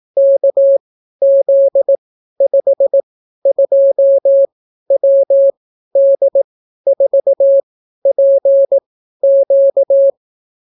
Morse code, heard as a keyed tone, about 18 wpm, spaced as for 12 wpm, 550 Hz. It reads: KZ52WD4PQ